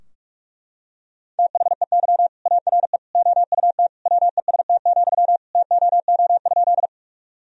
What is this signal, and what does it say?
Morse code, heard as a keyed tone, about 43 wpm, 700 Hz. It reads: THEY ARE OUT WEST, TOO?
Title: Sentences from Top 1000 Words - Review Part 2 - 45wpm